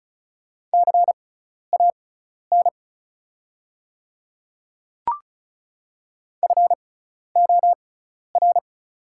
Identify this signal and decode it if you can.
Morse code, heard as a keyed tone, about 36 wpm, spaced as for 12 wpm, 690 Hz. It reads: CAN FOR
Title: 6x Character Spacing - Top 200 Words - 35wpm